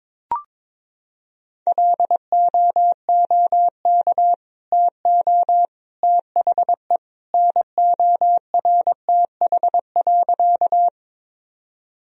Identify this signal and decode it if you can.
Morse code, heard as a keyed tone, about 22 wpm, 700 Hz. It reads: LOOK TO THE NORTH.
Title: Sentences from Top 1000 Words - Review Part 1 - 22wpm